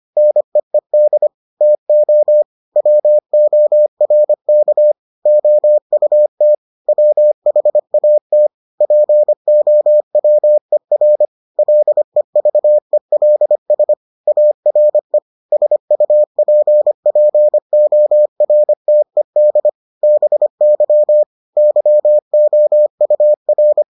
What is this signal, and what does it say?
Morse code, heard as a keyed tone, about 25 wpm, 600 Hz. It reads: NEED TO WORK OUT WHAT POWER LEVELS ARE SUPPORTED BY YOUR